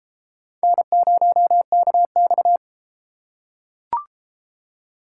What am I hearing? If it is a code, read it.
Morse code, heard as a keyed tone, about 32 wpm, 700 Hz. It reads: N0KX